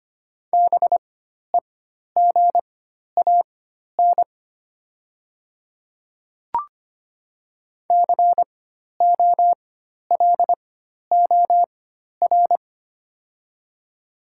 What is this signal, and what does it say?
Morse code, heard as a keyed tone, about 25 wpm, spaced as for 12 wpm, 700 Hz. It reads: BEGAN COLOR
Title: 4x Character Spacing - Top 300 Words - 25wpm